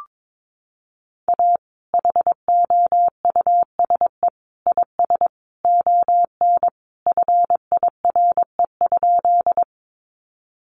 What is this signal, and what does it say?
Morse code, heard as a keyed tone, about 22 wpm, 700 Hz. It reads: A HOUSE IS ON FIRE?
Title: Sentences from Top 400 Words 22wpm